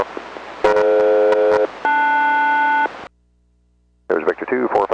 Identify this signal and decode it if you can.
Aircraft SELCAL, heard as two pairs of tones, with speech. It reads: DF-LS